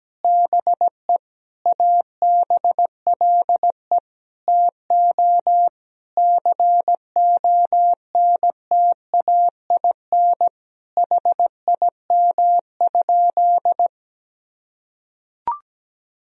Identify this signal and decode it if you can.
Morse code, heard as a keyed tone, about 17 wpm, 700 Hz. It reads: BE ABLE TO CONTAIN HIM?
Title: Sentences from Top 1000 Words - Review Part 2 - 17wpm